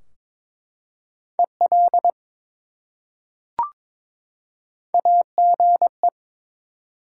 Morse code, heard as a keyed tone, about 22 wpm, 700 Hz. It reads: EL AGE
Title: Top 180 QSO Elements 22wpm